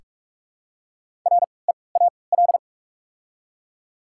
Morse code, heard as a keyed tone, about 46 wpm, spaced as for 25 wpm, 700 Hz. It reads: REAL
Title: 3x Character Spacing - Top 300 Words - 45wpm